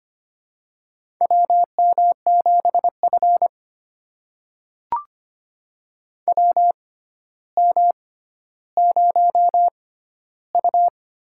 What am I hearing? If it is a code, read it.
Morse code, heard as a keyed tone, about 25 wpm, spaced as for 13 wpm, 700 Hz. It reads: WM7F WM0U
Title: USA Call Signs - 4 Characters - 6x Character Spacing - 25wpm